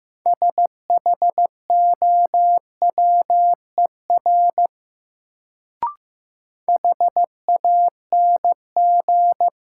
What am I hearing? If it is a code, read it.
Morse code, heard as a keyed tone, about 15 wpm, 700 Hz. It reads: SHOWER HANG
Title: Sentences from Top 1600 Words - New Word Drill - 15wpm